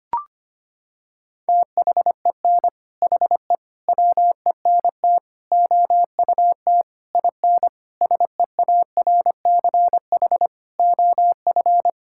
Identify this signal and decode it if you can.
Morse code, heard as a keyed tone, about 25 wpm, 700 Hz. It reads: THEN HE WENT OUT IN SEARCH OF
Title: Sentences from Top 1000 Words - Review Part 5 - 25wpm